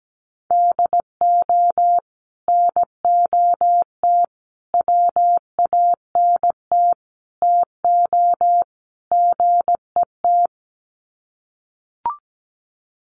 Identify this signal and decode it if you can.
Morse code, heard as a keyed tone, about 17 wpm, 700 Hz. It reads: DO NOT WANT TO GET
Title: Sets of 5 Words 17wpm